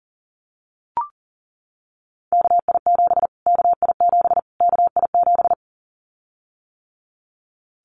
Morse code, heard as a keyed tone, about 40 wpm, 700 Hz. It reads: KI7 KI7 KI7
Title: Callsign Prefixes for Top 25 DXCC - Repeated 3x - 40wpm